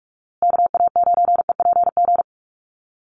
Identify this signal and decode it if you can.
Morse code, heard as a keyed tone, about 44 wpm, 700 Hz. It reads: KA9EPG
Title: USA Call Signs Repeated 3x 45wpm